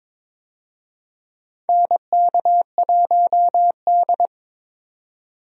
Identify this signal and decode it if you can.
Morse code, heard as a keyed tone, about 22 wpm, 700 Hz. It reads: NK1D
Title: USA Call Signs - 4 Characters - 22wpm